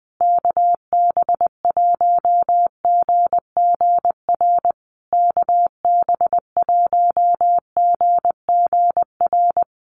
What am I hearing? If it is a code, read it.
Morse code, heard as a keyed tone, about 20 wpm, 700 Hz. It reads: KB1GGR KB1GGR
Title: USA Call Signs Repeated 3x 20wpm